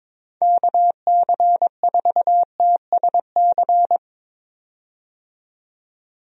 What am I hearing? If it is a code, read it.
Morse code, heard as a keyed tone, about 21 wpm, 700 Hz. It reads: KC4TSC
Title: USA Call Signs - 6 Characters - 22wpm